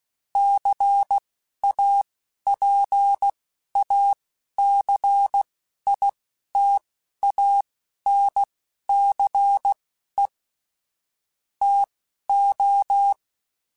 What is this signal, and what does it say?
Morse code, heard as a keyed tone, about 16 wpm, spaced as for 11 wpm, 800 Hz. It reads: CAPACITANCE TO
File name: Technical Subjects / Technical Subjects-mp3s / XC @10